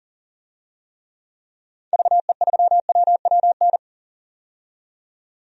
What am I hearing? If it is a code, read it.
Morse code, heard as a keyed tone, about 40 wpm, 700 Hz. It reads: VE3WWN
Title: DXCC-100 Call Signs - 6 Characters - 40wpm